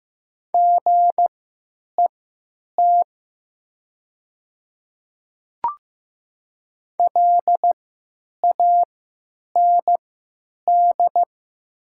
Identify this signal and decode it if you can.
Morse code, heard as a keyed tone, about 15 wpm, spaced as for 9 wpm, 700 Hz. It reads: GET LAND
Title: 3x Character Spacing - Top 200 Words - 15wpm